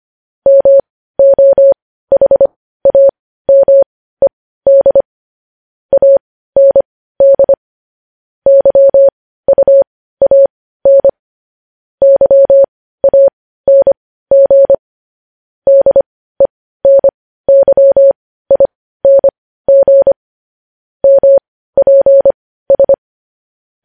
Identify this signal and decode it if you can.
Morse code, heard as a keyed tone, about 25 wpm, spaced as for 15 wpm, 550 Hz. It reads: MOHAMED AND YUAN YANG DENYING MPS
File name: podcast / News headlines-1x-25x15